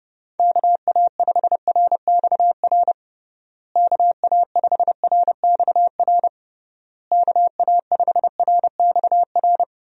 Morse code, heard as a keyed tone, about 32 wpm, 700 Hz. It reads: KA5RXR KA5RXR KA5RXR
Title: USA Call Signs - 6 Characters - Repeated 3x - 30wpm